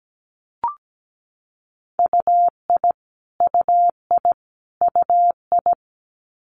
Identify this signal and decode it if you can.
Morse code, heard as a keyed tone, about 17 wpm, 700 Hz. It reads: UI UI UI
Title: Multi-letter Phonemes Repeated 3x 17wpm